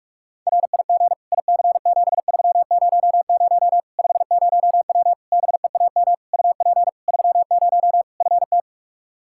Morse code, heard as a keyed tone, about 42 wpm, 700 Hz. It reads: RIG IC7300 50W BEAM UP 30 FT
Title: Practice QSO - Exchange 2 - Info - 45wpm